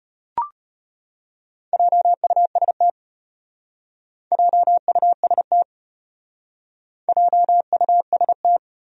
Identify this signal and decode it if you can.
Morse code, heard as a keyed tone, about 32 wpm, 700 Hz. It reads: JUST JUST JUST